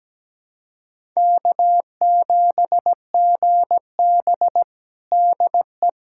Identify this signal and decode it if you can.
Morse code, heard as a keyed tone, about 17 wpm, 700 Hz. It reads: K7GB DE
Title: USA Callsign DE Callsign - 4 Characters - 17wpm